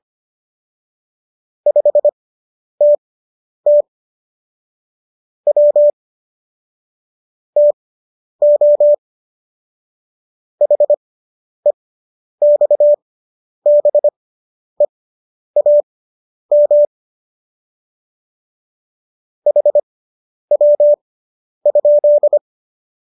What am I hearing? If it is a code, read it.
Morse code, heard as a keyed tone, about 25 wpm, spaced as for 10 wpm, 590 Hz. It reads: 5TT W TO HEXBEAM HW?